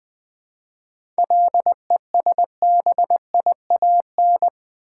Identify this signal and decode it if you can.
Morse code, heard as a keyed tone, about 20 wpm, 700 Hz. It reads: LESBIAN
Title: 7 Letter Words - First 3 Letters Given - 20wpm